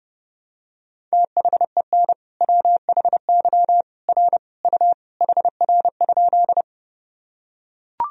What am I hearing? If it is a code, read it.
Morse code, heard as a keyed tone, about 30 wpm, 700 Hz. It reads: THEN WHY R U HR?
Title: Sentences from Top 200 Words and Abbreviations 30wpm